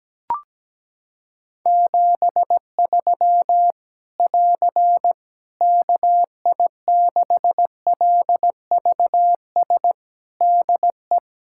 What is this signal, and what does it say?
Morse code, heard as a keyed tone, about 17 wpm, 700 Hz. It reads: 73 <AR> KI6LVS DE
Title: Practice QSO - Exchange 4 - Ending - 17wpm